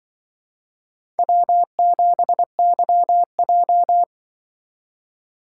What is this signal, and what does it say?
Morse code, heard as a keyed tone, about 25 wpm, 700 Hz. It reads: W7YJ
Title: Speed Racing - USA 1x2 Call Signs 20wpm